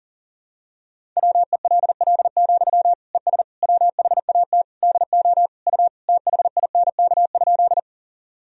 Morse code, heard as a keyed tone, about 40 wpm, 700 Hz. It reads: WELL, ES WHAT DO U THINK?
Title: Sentences from Top 200 Words and Abbreviations 40wpm